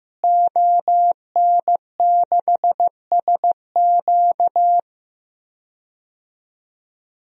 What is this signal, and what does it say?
Morse code, heard as a keyed tone, about 16 wpm, 700 Hz. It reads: ON6SQ
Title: DXCC-100 Call Signs - 5 Characters - Repeated 3x - 15wpm